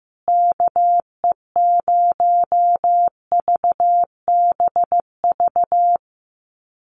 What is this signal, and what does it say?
Morse code, heard as a keyed tone, about 15 wpm, 700 Hz. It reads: KE0VBV
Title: USA Call Signs Repeated 3x 15wpm